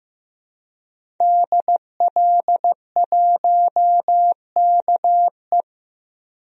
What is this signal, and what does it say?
Morse code, heard as a keyed tone, about 15 wpm, 700 Hz. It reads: DL1KE
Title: DXCC-100 Call Signs - 5 Characters - 15wpm